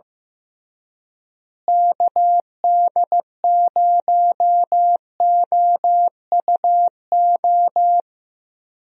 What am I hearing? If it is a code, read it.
Morse code, heard as a keyed tone, about 15 wpm, 700 Hz. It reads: KD0OUO
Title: USA Call Signs - 6 Characters - Repeated 3x - 15wpm